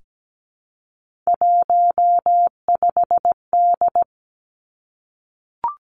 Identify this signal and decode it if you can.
Morse code, heard as a keyed tone, about 17 wpm, 700 Hz. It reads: 15D
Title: Three Challenging Letter-Number Combinations 17wpm